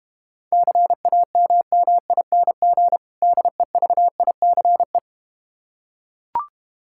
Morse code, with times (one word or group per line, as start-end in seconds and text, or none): 0.5-3.0 CAMMING
3.2-5.0 DEVICE